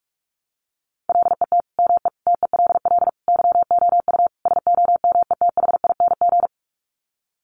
Morse code, heard as a keyed tone, about 44 wpm, 700 Hz. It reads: LET ME TELL YOU SOMETHING
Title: Sets of 5 Words 45wpm